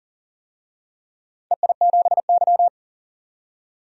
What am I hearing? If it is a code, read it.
Morse code, heard as a keyed tone, about 40 wpm, 700 Hz. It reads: EI7Y
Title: DXCC-100 Call Signs - 4 Characters - 40wpm